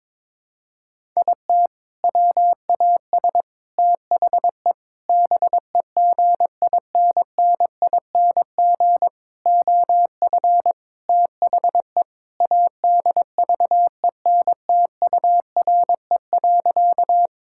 Morse code, heard as a keyed tone, about 22 wpm, 700 Hz. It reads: IT WAS THE BEGINNING OF THE ADVENTURE.
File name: practice / 22wpm / Sentences with 4-Word Phrases - 22wpm